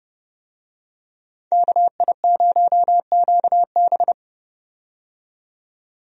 Morse code, heard as a keyed tone, about 30 wpm, 700 Hz. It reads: KI0QB
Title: USA Call Signs - 5 Characters - 30wpm